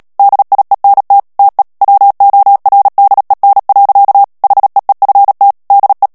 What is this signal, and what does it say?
Morse code, heard as a keyed tone, about 37 wpm, 800 Hz. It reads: DIENT TE WORDEN. HEEFT DE
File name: morse mp3 / HOLLANDAIS texte clair 2 vitesse 35 mots minute